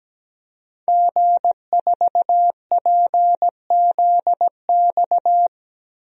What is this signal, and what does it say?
Morse code, heard as a keyed tone, about 17 wpm, 700 Hz. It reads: G4PZX
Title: DXCC-100 Call Signs - 5 Characters - Repeated 3x - 17wpm